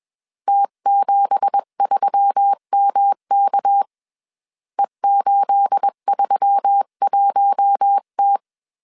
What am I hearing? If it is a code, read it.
Morse code, heard as a keyed tone, about 21 wpm, 800 Hz. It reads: T73MK E831T